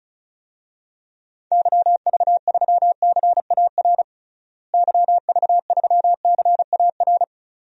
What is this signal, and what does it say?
Morse code, heard as a keyed tone, about 36 wpm, 700 Hz. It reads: YV3CAR YV3CAR
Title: DXCC-100 Call Signs - 6 Characters - Repeated 2x - 35wpm